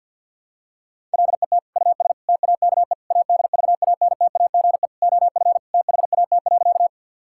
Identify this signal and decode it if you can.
Morse code, heard as a keyed tone, about 50 wpm, 700 Hz. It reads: LET US TAKE ADVANTAGE OF THAT.
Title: Sentences from Top 1000 Words - Review Part 6 - 50wpm